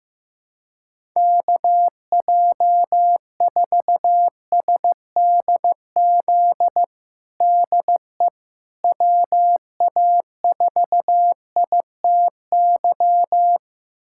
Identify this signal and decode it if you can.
Morse code, heard as a keyed tone, about 15 wpm, 700 Hz. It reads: KJ4SDZ DE WA4ITY
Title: USA Callsign DE Callsign - 6 Characters - 15wpm